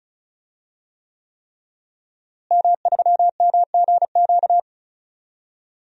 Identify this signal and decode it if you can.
Morse code, heard as a keyed tone, about 36 wpm, 700 Hz. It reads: M3MGQ
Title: DXCC-100 Call Signs - 5 Characters - Repeated 3x - 35wpm